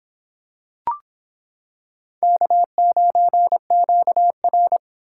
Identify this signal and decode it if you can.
Morse code, heard as a keyed tone, about 25 wpm, 700 Hz. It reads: K9QR